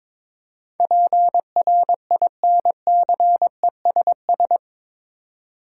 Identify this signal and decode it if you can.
Morse code, heard as a keyed tone, about 22 wpm, 700 Hz. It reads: PRINCESS
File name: practice / 22wpm / Eight Letter Words - 22wpm